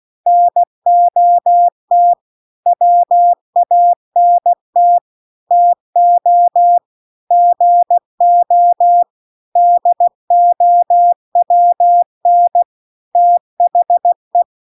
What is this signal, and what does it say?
Morse code, heard as a keyed tone, about 16 wpm, 700 Hz. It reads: NOT WANT TO GO DOWN THE
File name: CW-The-Big-Twoheaded-River-2-16WPM